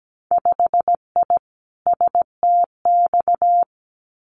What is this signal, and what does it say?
Morse code, heard as a keyed tone, about 17 wpm, 700 Hz. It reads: 5I STX
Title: Winter Field Day Exchanges Encore 17wpm